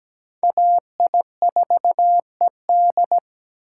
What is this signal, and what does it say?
Morse code, heard as a keyed tone, about 20 wpm, 700 Hz. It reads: AI4ED
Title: Speed Racing - USA 2x2 Call Signs 17wpm